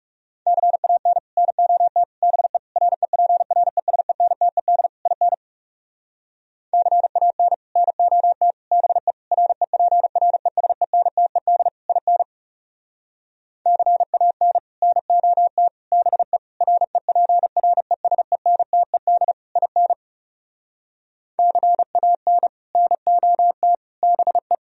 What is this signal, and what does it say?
Morse code, heard as a keyed tone, about 39 wpm, 700 Hz. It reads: CAN NOT BE REPRESENTED IN CAN NOT BE REPRESENTED IN CAN NOT BE REPRESENTED IN CAN NOT BE